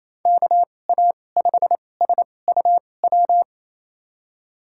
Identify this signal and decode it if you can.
Morse code, heard as a keyed tone, about 28 wpm, spaced as for 20 wpm, 700 Hz. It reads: KA5SUW
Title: USA Call Signs - 2x Character Spacing - 28wpm